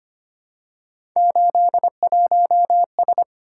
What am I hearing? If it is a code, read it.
Morse code, heard as a keyed tone, about 25 wpm, 700 Hz. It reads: 81S
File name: Three Letter-Number Combinations - 25wpm